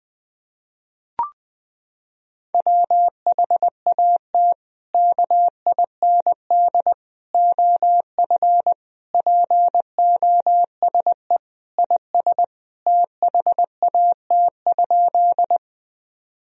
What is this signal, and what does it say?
Morse code, heard as a keyed tone, about 20 wpm, 700 Hz. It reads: WHAT KIND OF POSE IS THAT?
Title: Sentences from Top 1000 Words - Review Part 2 - 20wpm